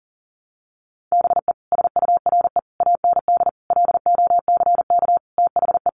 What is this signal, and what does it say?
Morse code, heard as a keyed tone, about 40 wpm, 700 Hz. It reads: BE SURE AND LOCK THE
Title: Sentences from Top 1300 Words 40wpm